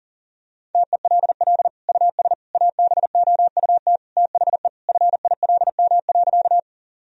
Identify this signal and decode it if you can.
Morse code, heard as a keyed tone, about 40 wpm, 700 Hz. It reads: TELL US ABOUT THE FILM.
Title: Sentences from Top 1000 Words - Review Part 3 - 40wpm